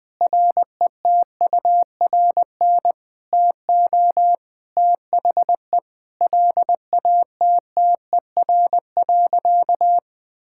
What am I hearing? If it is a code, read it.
Morse code, heard as a keyed tone, about 21 wpm, 700 Hz. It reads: RETURN TO THE LATTER.
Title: Sentences from Top 1600 Words 20wpm